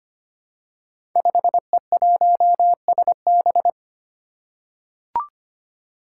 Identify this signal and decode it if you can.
Morse code, heard as a keyed tone, about 25 wpm, 700 Hz. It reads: 5E1SB
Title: Five Challenging Letter-Number Combinations 25wpm